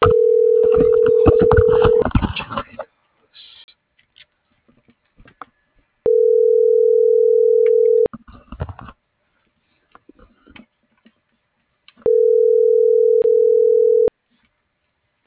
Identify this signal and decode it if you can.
Ringback tone (North America), 440 + 480 Hz.